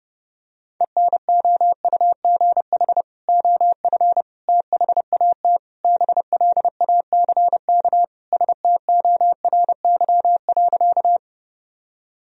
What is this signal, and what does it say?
Morse code, heard as a keyed tone, about 32 wpm, 700 Hz. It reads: ENOUGH OF THAT BLACK STORY.
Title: Sentences from Top 1000 Words - Review Part 2 - 30wpm